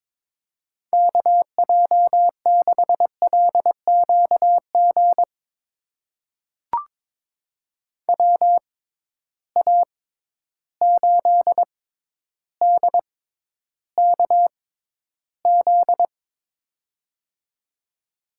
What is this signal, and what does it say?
Morse code, heard as a keyed tone, about 22 wpm, spaced as for 11 wpm, 700 Hz. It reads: KJ6LQG WA8DKZ
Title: USA Call Signs - 6 Characters - 6x Character Spacing - 22wpm